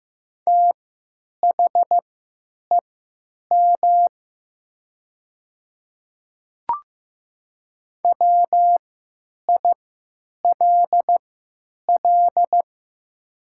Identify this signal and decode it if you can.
Morse code, heard as a keyed tone, about 15 wpm, spaced as for 9 wpm, 700 Hz. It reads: THEM WILL
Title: 3x Character Spacing - Top 100 Words - 15wpm